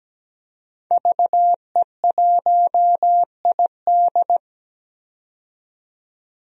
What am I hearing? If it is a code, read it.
Morse code, heard as a keyed tone, about 17 wpm, 700 Hz. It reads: VE1ID
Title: DXCC-100 Call Signs - 5 Characters - 17wpm